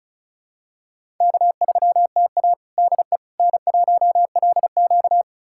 Morse code, heard as a keyed tone, about 34 wpm, 700 Hz. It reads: K3TA DE N1LQ
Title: USA Callsign DE Callsign - 4 Characters - 35wpm